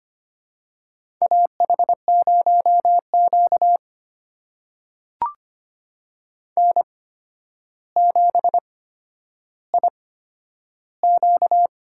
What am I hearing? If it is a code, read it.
Morse code, heard as a keyed tone, about 25 wpm, spaced as for 11 wpm, 700 Hz. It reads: AH0Q N7IQ
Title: USA Call Signs - 4 Characters - 8x Character Spacing - 25wpm